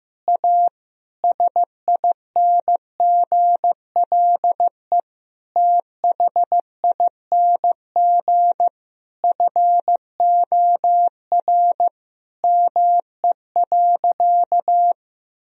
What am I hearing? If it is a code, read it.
Morse code, heard as a keyed tone, about 15 wpm, 700 Hz. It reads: A SINGLE THING FOR ME.
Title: Sentences from Top 1000 Words - Review Part 4 - 15wpm